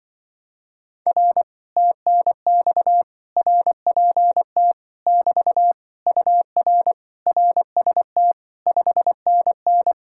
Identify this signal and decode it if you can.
Morse code, heard as a keyed tone, about 25 wpm, 700 Hz. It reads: R TNX RPT <BT> UR RST 5NN